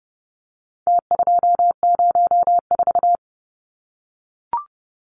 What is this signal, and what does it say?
Morse code, heard as a keyed tone, about 30 wpm, 700 Hz. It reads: T204
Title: Numbers - Contest Style - 4-Digit 30wpm